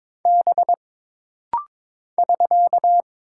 Morse code, heard as a keyed tone, about 22 wpm, 700 Hz. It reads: B <SK>
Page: This rapid-fire timing is geared to push your ICR (Instant Character Recognition) skills.